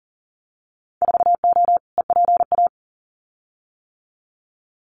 Morse code, heard as a keyed tone, about 38 wpm, 700 Hz. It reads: 4O EPA
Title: Winter Field Day Exchanges 40wpm